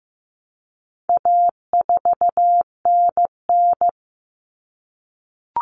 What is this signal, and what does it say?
Morse code, heard as a keyed tone, about 15 wpm, 700 Hz. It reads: A4NN
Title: Common Cut Numbers - 4-Digit 15wpm